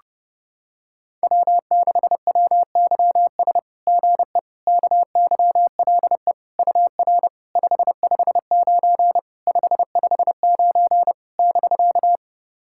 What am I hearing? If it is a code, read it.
Morse code, heard as a keyed tone, about 32 wpm, 690 Hz. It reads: W6WYS GE KYLE UR 559 559 <BK>
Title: SOTA QSO Practice 30wpm